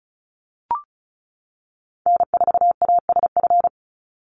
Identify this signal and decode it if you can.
Morse code, heard as a keyed tone, about 34 wpm, 700 Hz. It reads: N4ASF
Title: Call Signs - 35wpm